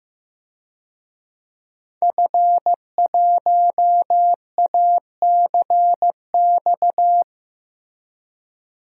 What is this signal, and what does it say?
Morse code, heard as a keyed tone, about 15 wpm, 700 Hz. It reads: F1ACX
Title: DXCC-100 Call Signs - 5 Characters - Repeated 3x - 15wpm